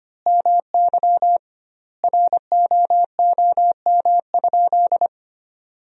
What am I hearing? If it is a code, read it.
Morse code, heard as a keyed tone, about 25 wpm, 700 Hz. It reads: MY ROOM?
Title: 2x Word-Spacing - Sentences from Top 300 Words - 25wpm